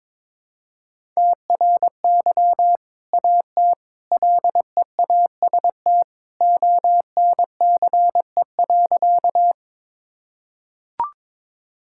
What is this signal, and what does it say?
Morse code, heard as a keyed tone, about 22 wpm, 700 Hz. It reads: TRY AT LEAST ONCE.
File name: practice / 22wpm / Sentences with 2-Word Phrases - 22wpm